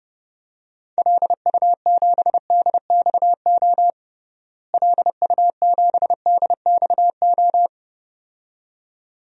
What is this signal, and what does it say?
Morse code, heard as a keyed tone, about 30 wpm, 700 Hz. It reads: LU7DXO LU7DXO
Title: DXCC-100 Call Signs - 6 Characters - Repeated 2x - 30wpm